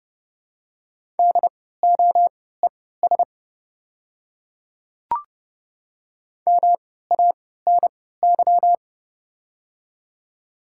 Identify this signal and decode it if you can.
Morse code, heard as a keyed tone, about 30 wpm, spaced as for 17 wpm, 700 Hz. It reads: DOES MANY
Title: 3x Character Spacing - Top 200 Words - 30wpm